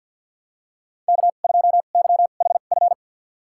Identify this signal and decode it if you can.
Morse code, heard as a keyed tone, about 47 wpm, spaced as for 35 wpm, 700 Hz. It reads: K2YHF
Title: USA Call Signs - 5 Characters - 2x Character Spacing - 50wpm